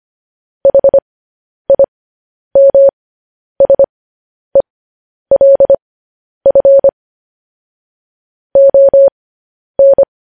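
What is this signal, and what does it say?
Morse code, heard as a keyed tone, about 25 wpm, spaced as for 10 wpm, 550 Hz. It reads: HIMSELF ON
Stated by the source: News headlines sent in Morse code at 10 wpm.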